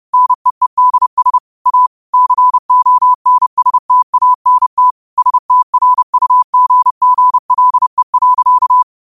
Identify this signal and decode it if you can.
Morse code, heard as a keyed tone, about 30 wpm, 1000 Hz. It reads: NEEDS A CONSTANT STRUGGLE.